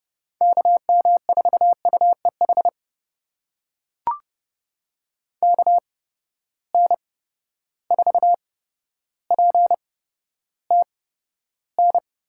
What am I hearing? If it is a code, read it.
Morse code, heard as a keyed tone, about 31 wpm, spaced as for 13 wpm, 700 Hz. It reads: KM4UEH KN4PTN
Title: USA Call Signs - 6 Characters - 8x Character Spacing - 30wpm